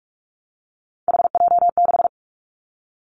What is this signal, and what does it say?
Morse code, heard as a keyed tone, about 44 wpm, 700 Hz. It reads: HJ6
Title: Callsign Prefixes for Top 100 DXCC - 45wpm